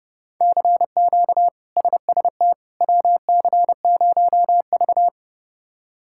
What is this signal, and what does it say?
Morse code, heard as a keyed tone, about 30 wpm, 700 Hz. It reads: CQ SST WC0V
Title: K1USN SST QSO Practice - 30wpm